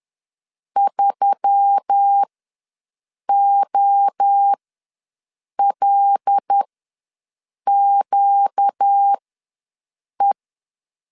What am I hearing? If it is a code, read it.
Morse code, heard as a keyed tone, about 10 wpm, spaced as for 6 wpm, 800 Hz. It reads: 3OLQE
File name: LEZIONI / 7 GF9Z / 7 GF9 ws 10wpm 4034